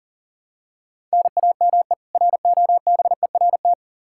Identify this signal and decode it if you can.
Morse code, heard as a keyed tone, about 37 wpm, 700 Hz. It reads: NAME ROBERT
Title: Practice QSO - Exchange 1 - Beginning - 40wpm